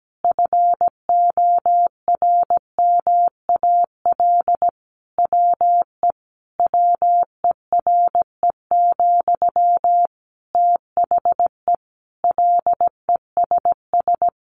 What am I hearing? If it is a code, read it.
Morse code, heard as a keyed tone, about 17 wpm, 700 Hz. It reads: FORMAL WE WERE, THE LESS
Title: Sentences from Top 1300 Words 17wpm